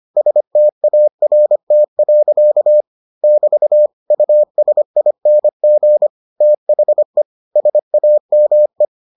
Morse code, heard as a keyed tone, about 25 wpm, 600 Hz. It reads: START. <BT> USING THE SAME